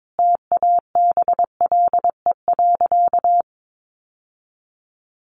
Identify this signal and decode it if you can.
Morse code, heard as a keyed tone, about 22 wpm, 700 Hz. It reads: TABLE.
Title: Sentences from Top 1200 Words 22wpm